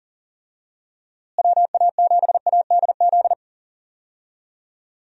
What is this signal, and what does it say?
Morse code, heard as a keyed tone, about 37 wpm, 700 Hz. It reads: WA7ADZ
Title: USA Call Signs - 6 Characters - Repeated 2x - 40wpm